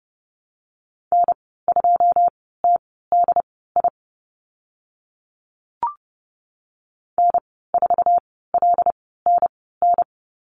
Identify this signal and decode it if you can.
Morse code, heard as a keyed tone, about 30 wpm, spaced as for 17 wpm, 700 Hz. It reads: N2TDI N4LNN
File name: Lesson 14g - Number 2 - Call Signs 30wpm